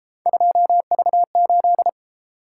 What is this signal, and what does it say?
Morse code, heard as a keyed tone, about 33 wpm, 700 Hz. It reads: 2V8